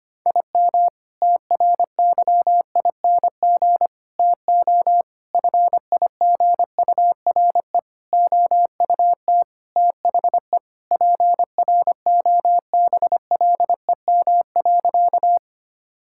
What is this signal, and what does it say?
Morse code, heard as a keyed tone, about 25 wpm, 700 Hz. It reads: IM TRYING TO FIGURE OUT THE PROBLEM.
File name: practice / 25wpm / Sentences with 4-Word Phrases - 25wpm